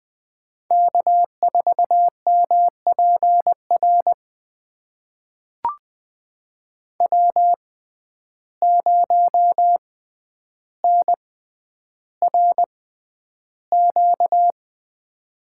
Morse code, heard as a keyed tone, about 20 wpm, spaced as for 10 wpm, 700 Hz. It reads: K4MPR W0NRQ
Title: USA Call Signs - 5 Characters - 6x Character Spacing - 20wpm